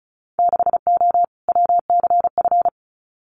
Morse code, heard as a keyed tone, about 36 wpm, 700 Hz. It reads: 6O WCF
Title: Winter Field Day Exchanges Encore 35wpm